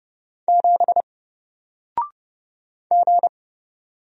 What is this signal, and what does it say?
Morse code, heard as a keyed tone, about 31 wpm, 700 Hz. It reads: 7 G
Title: Single Letter-Number Alt 30wpm